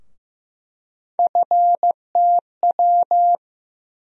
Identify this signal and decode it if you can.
Morse code, heard as a keyed tone, about 15 wpm, 700 Hz. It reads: FTW
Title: USA Call Sign Postfixes - 3 Letters 15wpm